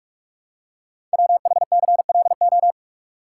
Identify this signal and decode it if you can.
Morse code, heard as a keyed tone, about 44 wpm, 700 Hz. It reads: WHCLO
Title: Code Groups - 5 Letter:Numbers - 45wpm